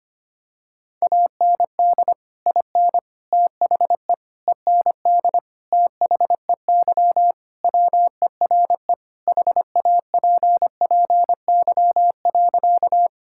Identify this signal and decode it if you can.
Morse code, heard as a keyed tone, about 25 wpm, 700 Hz. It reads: AND IN THE END THEY WERE HAPPY.
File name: practice / 25wpm / Sentences with 2-Word Phrases - 25wpm